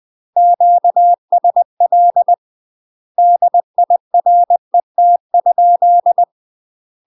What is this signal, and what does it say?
Morse code, heard as a keyed tone, about 20 wpm, 700 Hz. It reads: QSL DIRET?
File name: UK MASTER QSOs TRAINING / 20WPM / UK MASTER QSOs 0018